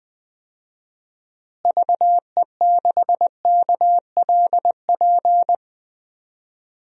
Morse code, heard as a keyed tone, about 19 wpm, 700 Hz. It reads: VE6KLP
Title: DXCC-100 Call Signs - 6 Characters - Repeated 2x - 20wpm